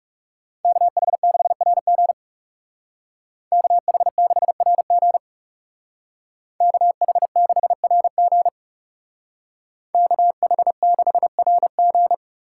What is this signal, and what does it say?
Morse code, heard as a keyed tone, about 37 wpm, 700 Hz. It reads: KH6RG KH6RG KH6RG KH6RG
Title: Speed Racing - USA Call Signs 30wpm